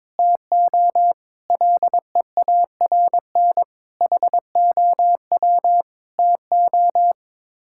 Morse code, heard as a keyed tone, about 22 wpm, 700 Hz. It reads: TO LEARN HOW TO
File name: practice / 22wpm / Sets of 6 Words - 22wpm